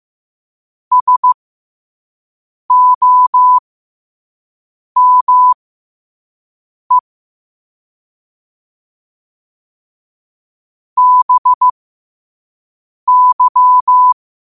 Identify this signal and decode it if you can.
Morse code, heard as a keyed tone, about 15 wpm, spaced as for 5 wpm, 1000 Hz. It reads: SOME BY
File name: qotdfiles / 5wpm